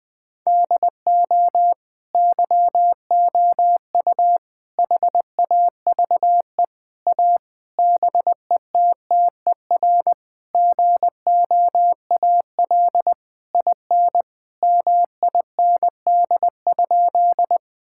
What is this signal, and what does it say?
Morse code, heard as a keyed tone, about 21 wpm, 700 Hz. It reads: DO YOU HAVE A BETTER GOAL IN MIND?
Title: Sentences from Top 1000 Words - Review Part 5 - 20wpm